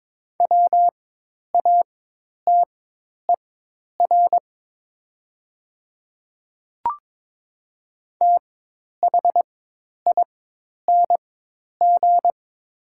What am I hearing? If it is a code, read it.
Morse code, heard as a keyed tone, about 22 wpm, spaced as for 10 wpm, 700 Hz. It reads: WATER THING
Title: 4x Character Spacing - Top 100 Words - 22wpm